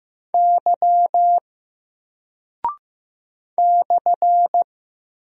Morse code, heard as a keyed tone, about 15 wpm, 700 Hz. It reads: Y /
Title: Single Character (Letter:Number:Punctuation:Prosign) Alt 15wpm